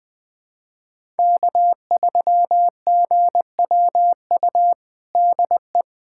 Morse code, heard as a keyed tone, about 20 wpm, 700 Hz. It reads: K3GWU DE
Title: USA Callsign DE Callsign - 5 Characters - 20wpm